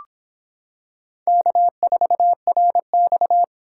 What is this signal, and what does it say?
Morse code, heard as a keyed tone, about 26 wpm, 700 Hz. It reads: K4RX